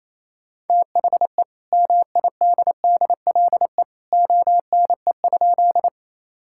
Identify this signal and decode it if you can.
Morse code, heard as a keyed tone, about 28 wpm, 700 Hz. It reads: THE MIDDLE ONE?
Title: Sentences from Top 1000 Words - Review Part 6 - 28wpm